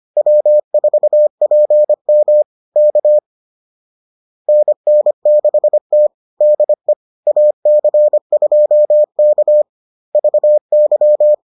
Morse code, heard as a keyed tone, about 25 wpm, 600 Hz. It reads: W4PM K NN6T DE AC2K VY